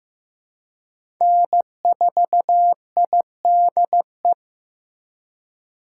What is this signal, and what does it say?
Morse code, heard as a keyed tone, about 15 wpm, 700 Hz. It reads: N4IDE
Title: USA Call Signs - 5 Characters - 3x Character Spacing - 15wpm